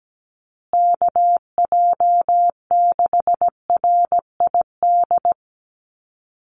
Morse code, heard as a keyed tone, about 17 wpm, 700 Hz. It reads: KJ6RID
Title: USA Call Signs Repeated 3x 17wpm